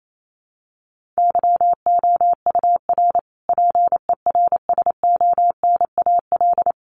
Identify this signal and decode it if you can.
Morse code, heard as a keyed tone, about 28 wpm, 700 Hz. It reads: YOUR PERSONAL
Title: A Sound Of Thunder 28wpm